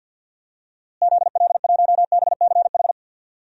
Morse code, heard as a keyed tone, about 52 wpm, 700 Hz. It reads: ZL1BCH
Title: DXCC-100 Call Signs - 6 Characters - Repeated 2x - 50wpm